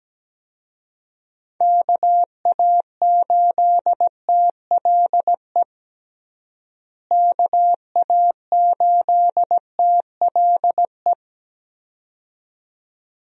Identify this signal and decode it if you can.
Morse code, heard as a keyed tone, about 17 wpm, 700 Hz. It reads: KA8TLE KA8TLE
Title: USA Call Signs - 6 Characters - Repeated 2x - 17wpm